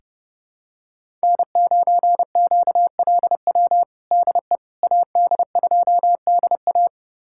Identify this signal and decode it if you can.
Morse code, heard as a keyed tone, about 30 wpm, 700 Hz. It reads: N9QLW DE AD2DA
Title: USA Callsign DE Callsign - 5 Characters - 30wpm